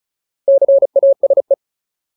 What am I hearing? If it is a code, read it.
Morse code, heard as a keyed tone, about 35 wpm, 550 Hz. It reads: CASE
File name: Ultimate Word List / 35WPM Ultimate